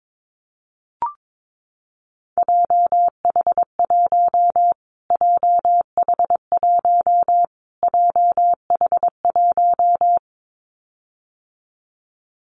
Morse code, heard as a keyed tone, about 22 wpm, 700 Hz. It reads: JH1 JH1 JH1
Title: Callsign Prefixes for Top 25 DXCC - Repeated 3x - 22wpm